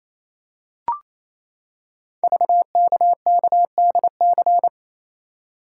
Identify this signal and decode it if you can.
Morse code, heard as a keyed tone, about 28 wpm, 700 Hz. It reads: VKKDC
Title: Code Groups - 5 Letters - 28wpm